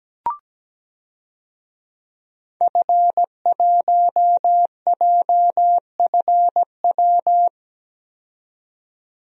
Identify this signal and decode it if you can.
Morse code, heard as a keyed tone, about 17 wpm, 700 Hz. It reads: F1JFW
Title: DXCC-100 Call Signs - 5 Characters - 17wpm